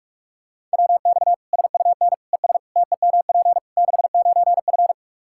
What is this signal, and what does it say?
Morse code, heard as a keyed tone, about 41 wpm, 700 Hz. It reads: WX SUN ES TEMP 69F
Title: K1USN SST QSO Practice 45wpm